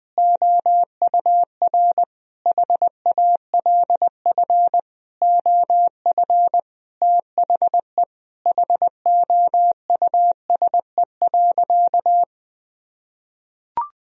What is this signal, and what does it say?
Morse code, heard as a keyed tone, about 20 wpm, 700 Hz. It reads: OUR HALF OF THE HOUSE.
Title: Sentences from Top 400 Words and Abbreviations 20wpm